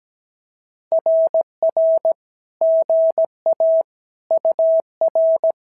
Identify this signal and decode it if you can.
Morse code, heard as a keyed tone, about 17 wpm, 650 Hz. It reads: RR GA UR
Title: SOTA QSO Practice 17wpm